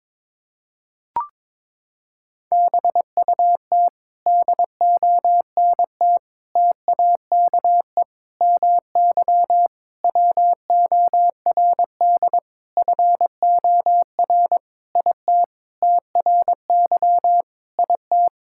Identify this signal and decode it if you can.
Morse code, heard as a keyed tone, about 22 wpm, 700 Hz. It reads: BUT DONT TAKE MY WORD FOR IT TRY IT
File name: Sentences with 7-Word Phrases - 22wpm